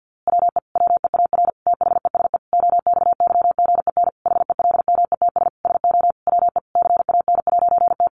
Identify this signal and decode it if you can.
Morse code, heard as a keyed tone, about 50 wpm, 700 Hz. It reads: WE WEAR THESE OXYGEN HELMETS SO WE CAN'T
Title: A Sound Of Thunder 50wpm